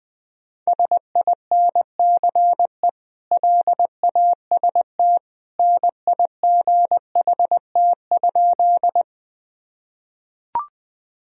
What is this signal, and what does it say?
Morse code, heard as a keyed tone, about 20 wpm, 700 Hz. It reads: SINCE LAST NIGHT?
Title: Sentences from Top 1000 Words - Review Part 1 - 20wpm